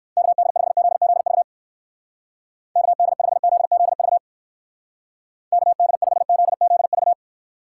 Morse code, heard as a keyed tone, about 56 wpm, 700 Hz. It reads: KD5ZZV KD5ZZV KD5ZZV
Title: Speed Racing - USA 2x3 Call Signs 45wpm